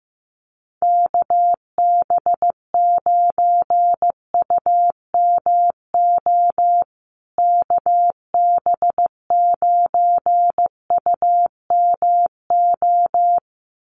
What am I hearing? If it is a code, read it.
Morse code, heard as a keyed tone, about 15 wpm, 700 Hz. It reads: KB9UMO KB9UMO
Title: USA Call Signs Repeated 2x 15wpm